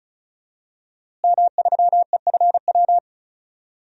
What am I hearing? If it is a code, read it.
Morse code, heard as a keyed tone, about 34 wpm, 700 Hz. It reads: M3EFW
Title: DXCC-100 Call Signs - 5 Characters - Repeated 2x - 35wpm